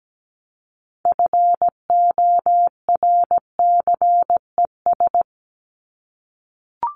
Morse code, heard as a keyed tone, about 17 wpm, 700 Hz. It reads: FORCES
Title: ES:S Suffix - Top 100 Words with Base Words 17wpm